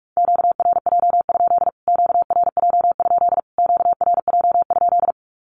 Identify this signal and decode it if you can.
Morse code, heard as a keyed tone, about 45 wpm, 700 Hz. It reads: QRJ? QRJ? QRJ?
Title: Q-Codes Repeated 3x 45wpm